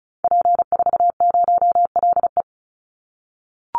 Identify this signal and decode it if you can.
Morse code, heard as a keyed tone, about 34 wpm, 700 Hz. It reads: P40LE
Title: Call Signs - 35wpm